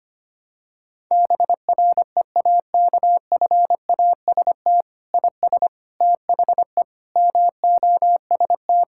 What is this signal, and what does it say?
Morse code, heard as a keyed tone, about 25 wpm, 700 Hz. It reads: BREAKFAST IS THE MOST
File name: Sets of 9 Words - 25wpm